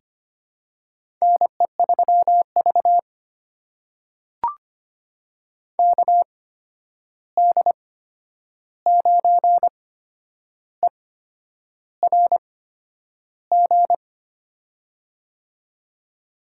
Morse code, heard as a keyed tone, about 25 wpm, spaced as for 9 wpm, 700 Hz. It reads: NE3V KD9ERG
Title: USA Call Signs - 8x Character Spacing - 25wpm